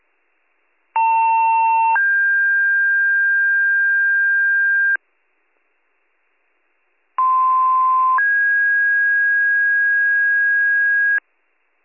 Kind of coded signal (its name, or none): two-tone page